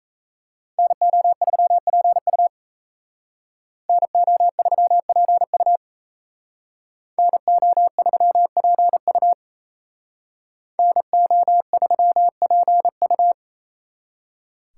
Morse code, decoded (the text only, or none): NO3PU NO3PU NO3PU NO3PU